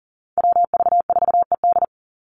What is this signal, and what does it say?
Morse code, heard as a keyed tone, about 40 wpm, 700 Hz. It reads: WV4ED
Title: USA Call Signs 40wpm